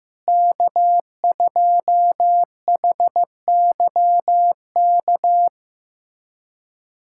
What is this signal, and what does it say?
Morse code, heard as a keyed tone, about 15 wpm, 700 Hz. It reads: K2HYK
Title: USA Callsign DE Callsign 15wpm